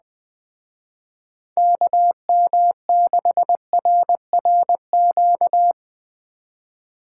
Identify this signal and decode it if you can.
Morse code, heard as a keyed tone, about 19 wpm, 700 Hz. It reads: KM6RRQ
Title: Speed Racing - USA Call Signs 17wpm